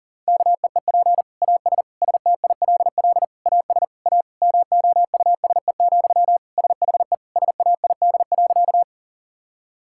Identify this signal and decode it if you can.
Morse code, heard as a keyed tone, about 40 wpm, 700 Hz. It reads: KEEP AS STILL AS A MOUSE, SHE SAID.
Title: Sentences from Top 1000 Words - Review Part 5 - 40wpm